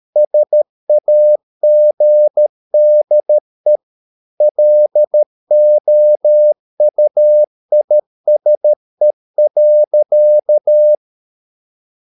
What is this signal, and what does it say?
Morse code, heard as a keyed tone, about 13 wpm, 600 Hz. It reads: SAGDE LOUISE.
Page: Ferie 13 wpm